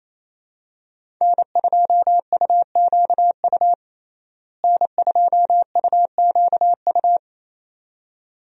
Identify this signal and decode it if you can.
Morse code, heard as a keyed tone, about 28 wpm, 700 Hz. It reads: N2UQU N2UQU
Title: USA Call Signs - 5 Characters - Repeated 2x - 28wpm